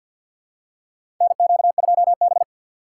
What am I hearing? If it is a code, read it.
Morse code, heard as a keyed tone, about 47 wpm, 700 Hz. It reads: NQ2B
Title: USA Call Signs - 4 Characters - 2x Character Spacing - 50wpm